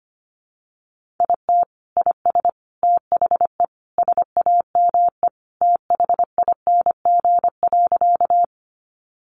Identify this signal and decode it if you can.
Morse code, heard as a keyed tone, about 25 wpm, 700 Hz. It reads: IT IS THE SAME THING.
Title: Sentences from Top 200 Words 25wpm